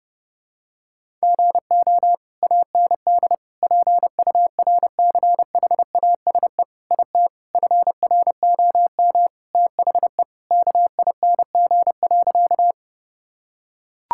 Morse code, heard as a keyed tone, about 32 wpm, 700 Hz. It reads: GO AND PURCHASE IT FROM THE KING.
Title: Sentences from Top 1000 Words - Review Part 5 - 30wpm